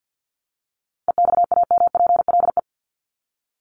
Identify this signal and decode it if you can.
Morse code, heard as a keyed tone, about 54 wpm, 700 Hz. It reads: EXAMPLE
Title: Top 500 Words 50wpm